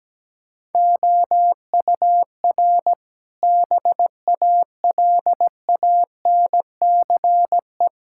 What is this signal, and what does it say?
Morse code, heard as a keyed tone, about 17 wpm, 700 Hz. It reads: OUR BALANCE
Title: Sentences from Top 1000 Words - Review Part 6 - 17wpm